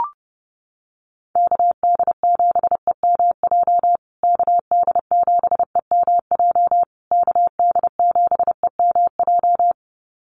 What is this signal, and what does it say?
Morse code, heard as a keyed tone, about 30 wpm, 700 Hz. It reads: KD7EMJ KD7EMJ KD7EMJ
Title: USA Call Signs Repeated 3x 30wpm